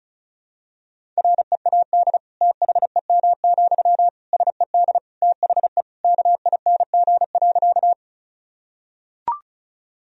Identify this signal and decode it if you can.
Morse code, heard as a keyed tone, about 34 wpm, 700 Hz. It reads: READ THEM, SED THE KING.
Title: Sentences from Top 400 Words and Abbreviations 35wpm